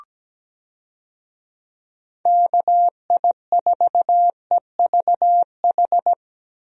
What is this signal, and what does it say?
Morse code, heard as a keyed tone, about 17 wpm, 700 Hz. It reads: KI4EVH
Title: USA Call Signs - 6 Characters - Repeated 3x - 17wpm